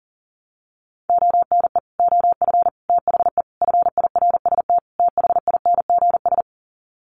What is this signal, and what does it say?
Morse code, heard as a keyed tone, about 37 wpm, 700 Hz. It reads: ONE OF THE FIRST THINGS
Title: Sets of 5 Words Encore 40wpm